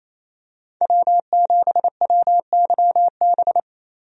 Morse code, heard as a keyed tone, about 28 wpm, 700 Hz. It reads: W7WYB
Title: USA Call Signs - 5 Characters - 28wpm